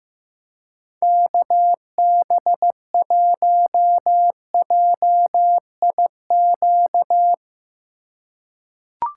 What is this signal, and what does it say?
Morse code, heard as a keyed tone, about 15 wpm, 700 Hz. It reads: KB1JIQ
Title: USA Call Signs - 8x Character Spacing - 15wpm